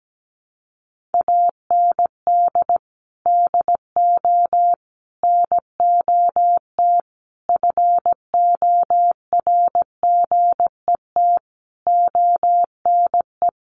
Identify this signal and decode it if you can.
Morse code, heard as a keyed tone, about 17 wpm, 700 Hz. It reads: AND DO NOT FORGET ONE
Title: Sentences from Top 1200 Words 17wpm